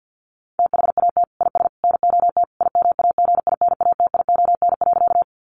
Morse code, heard as a keyed tone, about 50 wpm, 700 Hz. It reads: THAT IS NOT IMAGINATION.
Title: Sentences from Top 1300 Words 50wpm